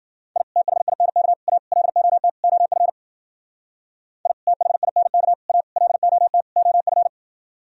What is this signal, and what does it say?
Morse code, heard as a keyed tone, about 53 wpm, 700 Hz. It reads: I THINK A LOT OF I THINK A LOT OF